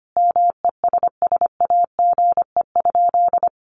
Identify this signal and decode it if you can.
Morse code, heard as a keyed tone, about 25 wpm, 700 Hz. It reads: MESSAGE?
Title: Sentences from Top 1100 Words 25wpm